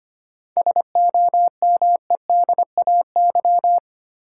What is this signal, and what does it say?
Morse code, heard as a keyed tone, about 25 wpm, 700 Hz. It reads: SOMEDAY
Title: Sentences from Top 1600 Words - New Word Drill Alt - 25wpm